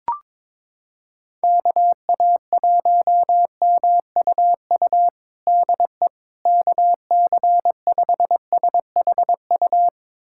Morse code, heard as a keyed tone, about 22 wpm, 700 Hz. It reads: KA1MUU DE KC5SHU
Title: USA Callsign DE Callsign - 6 Characters - 22wpm